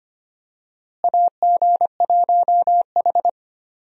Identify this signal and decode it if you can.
Morse code, heard as a keyed tone, about 25 wpm, 700 Hz. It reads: AG1H
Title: USA Call Signs - 4 Characters - Repeated 2x - 25wpm